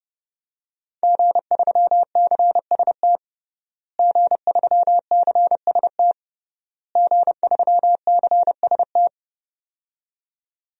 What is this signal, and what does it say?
Morse code, heard as a keyed tone, about 32 wpm, 700 Hz. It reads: G3CST G3CST G3CST
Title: DXCC-100 Call Signs - 5 Characters - Repeated 3x - 30wpm